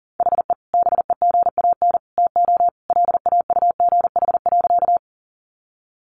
Morse code, heard as a keyed tone, about 40 wpm, 700 Hz. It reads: HE BEGAN TO LAUGH.
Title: Sentences from Top 500 Words 40wpm